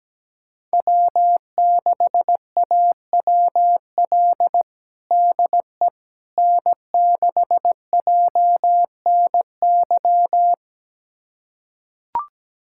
Morse code, heard as a keyed tone, about 17 wpm, 700 Hz. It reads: W6AWL DE N6JNY
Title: USA Callsign DE Callsign - 5 Characters - 17wpm